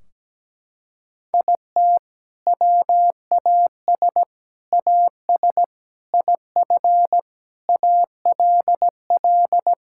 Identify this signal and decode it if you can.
Morse code, heard as a keyed tone, about 17 wpm, 700 Hz. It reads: IT WAS AS IF ALL
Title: Sentences from Top 1600 Words 17wpm